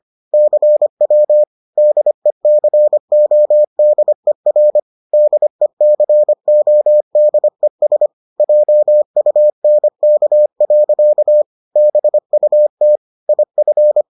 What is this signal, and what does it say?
Morse code, heard as a keyed tone, about 25 wpm, 600 Hz. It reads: CW DECODER DECODES JUNK. BUT IF